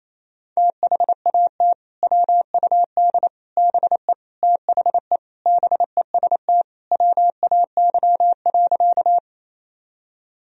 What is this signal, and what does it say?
Morse code, heard as a keyed tone, about 27 wpm, 700 Hz. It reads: THAT WUD BE THE BEST WAY.
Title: Sentences from Top 400 Words and Abbreviations 28wpm